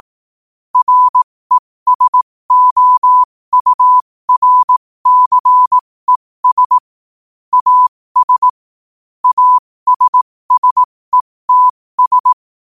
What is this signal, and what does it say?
Morse code, heard as a keyed tone, about 18 wpm, spaced as for 15 wpm, 1000 Hz. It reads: RESOURCES AS ASSETS
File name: qotdfiles / 15wpm